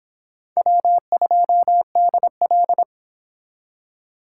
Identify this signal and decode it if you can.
Morse code, heard as a keyed tone, about 25 wpm, 700 Hz. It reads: W2DL